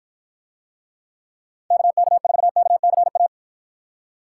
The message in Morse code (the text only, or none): KK4KCA